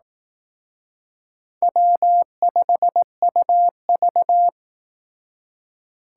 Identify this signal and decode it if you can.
Morse code, heard as a keyed tone, about 18 wpm, 700 Hz. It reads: W5UV
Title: Speed Racing - USA 1x2 Call Signs 15wpm